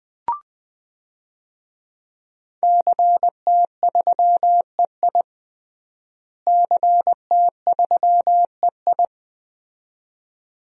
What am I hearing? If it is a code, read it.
Morse code, heard as a keyed tone, about 20 wpm, 700 Hz. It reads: CT3EI CT3EI
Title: DXCC-100 Call Signs - 5 Characters - Repeated 2x - 20wpm